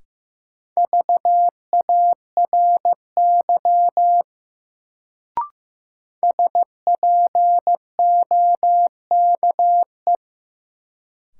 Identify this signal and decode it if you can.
Morse code, heard as a keyed tone, about 15 wpm, 700 Hz. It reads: VARY SPOKE
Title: Sentences from Top 1200 Words - New Word Drill - 15wpm